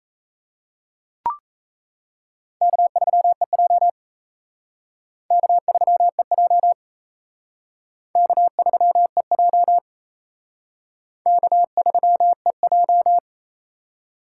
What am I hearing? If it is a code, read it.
Morse code, heard as a keyed tone, about 35 wpm, 700 Hz. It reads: K3EJ K3EJ K3EJ K3EJ
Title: Speed Racing - USA 1x2 Call Signs 28wpm